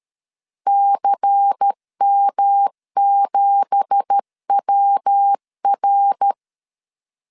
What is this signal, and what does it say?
Morse code, heard as a keyed tone, about 13 wpm, 800 Hz. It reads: CM7WR